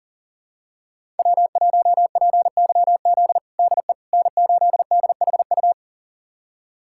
Morse code, heard as a keyed tone, about 37 wpm, 700 Hz. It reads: W1PYZ DE N8DHU
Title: USA Callsign DE Callsign - 5 Characters - 40wpm